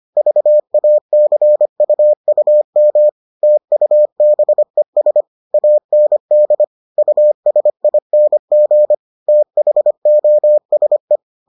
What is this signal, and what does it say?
Morse code, heard as a keyed tone, about 25 wpm, 600 Hz. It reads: VACUUM TUBES AND USING THOSE